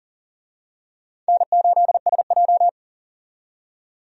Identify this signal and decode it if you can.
Morse code, heard as a keyed tone, about 40 wpm, 700 Hz. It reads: N8SJ
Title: USA Call Signs - 4 Characters - 40wpm